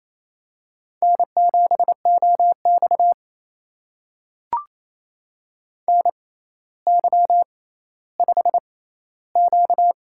Morse code, heard as a keyed tone, about 28 wpm, spaced as for 14 wpm, 700 Hz. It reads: N7OX NY5Q
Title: USA Call Signs - 4 Characters - 6x Character Spacing - 28wpm